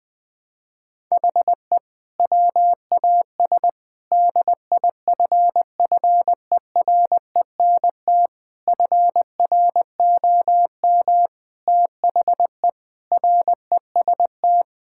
Morse code, heard as a keyed tone, about 20 wpm, 700 Hz. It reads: HE WAS DIFFERENT FROM THE REST
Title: Sentences with 5-Word Phrases - 20wpm